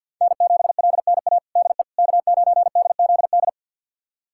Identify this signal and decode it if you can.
Morse code, heard as a keyed tone, about 48 wpm, 700 Hz. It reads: N7LNA DE K9DZD
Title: USA Callsign DE Callsign - 5 Characters - 50wpm